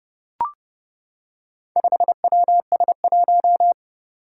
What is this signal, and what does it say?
Morse code, heard as a keyed tone, about 30 wpm, 700 Hz. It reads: 5WS1
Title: Four Challenging Letter-Number Combinations 30wpm